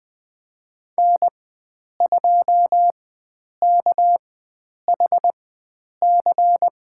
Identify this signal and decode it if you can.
Morse code, heard as a keyed tone, about 20 wpm, spaced as for 9 wpm, 700 Hz. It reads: N2KHC
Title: USA Call Signs - 5 Characters - 4x Character Spacing - 20wpm